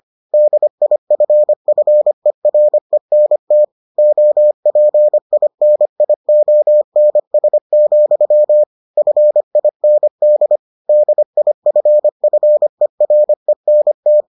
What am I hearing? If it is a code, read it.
Morse code, heard as a keyed tone, about 25 wpm, 600 Hz. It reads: DIFFERENT OPINIONS, FIND DIFFERENT